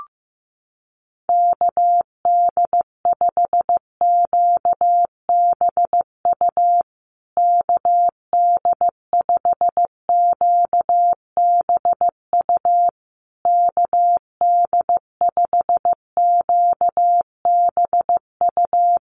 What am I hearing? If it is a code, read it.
Morse code, heard as a keyed tone, about 15 wpm, 700 Hz. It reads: KD5QBU KD5QBU KD5QBU
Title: USA Call Signs Repeated 3x 15wpm